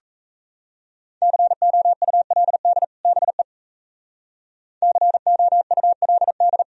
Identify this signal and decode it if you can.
Morse code, heard as a keyed tone, about 34 wpm, 700 Hz. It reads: COULD BE COULD
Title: Speed-Racing - Sets of 2 Words 28wpm